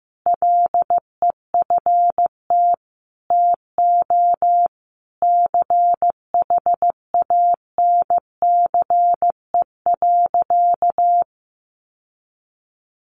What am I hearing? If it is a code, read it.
Morse code, heard as a keyed tone, about 15 wpm, 700 Hz. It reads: LEFT TO CHANCE.
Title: Sentences from Top 1100 Words 15wpm